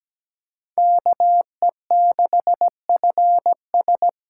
Morse code, heard as a keyed tone, about 17 wpm, 700 Hz. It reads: KE6FS
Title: USA Call Signs - 6 Characters - 8x Character Spacing - 17wpm